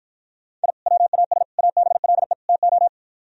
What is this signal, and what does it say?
Morse code, heard as a keyed tone, about 52 wpm, 700 Hz. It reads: I WAS ABLE TO